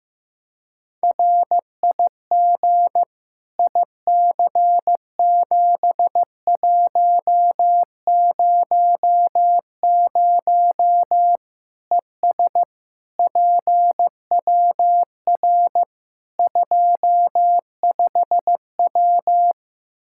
Morse code, heard as a keyed tone, about 15 wpm, 700 Hz. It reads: RIG IC7100 ES PWR 25W
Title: K1USN SST QSO Practice 15wpm